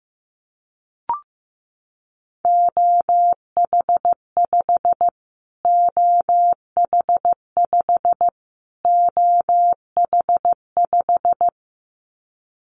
Morse code, heard as a keyed tone, about 15 wpm, 700 Hz. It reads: OH5 OH5 OH5
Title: Callsign Prefixes for Top 25 DXCC - Repeated 3x - 15wpm